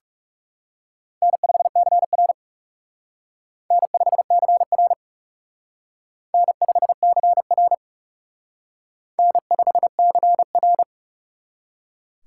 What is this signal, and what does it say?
Morse code, heard as a keyed tone, about 36 wpm, 700 Hz. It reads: N5CR N5CR N5CR N5CR
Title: Speed Racing - USA 1x2 Call Signs 30wpm